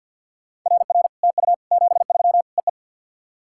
Morse code, heard as a keyed tone, about 48 wpm, 690 Hz. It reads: RR TU 73 EE
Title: SOTA QSO Practice 50wpm